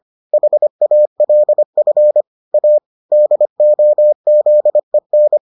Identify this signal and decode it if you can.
Morse code, heard as a keyed tone, about 25 wpm, 600 Hz. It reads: HALF A DOZEN